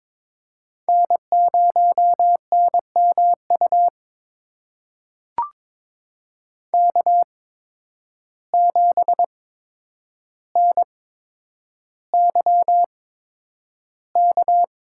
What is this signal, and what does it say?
Morse code, heard as a keyed tone, about 22 wpm, spaced as for 9 wpm, 700 Hz. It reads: N0NMU K7NYK
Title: USA Call Signs - 5 Characters - 8x Character Spacing - 22wpm